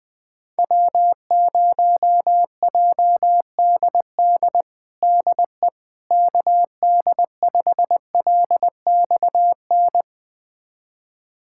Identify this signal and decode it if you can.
Morse code, heard as a keyed tone, about 20 wpm, 700 Hz. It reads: W0JDD DE KD5LXN
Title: USA Callsign DE Callsign 20wpm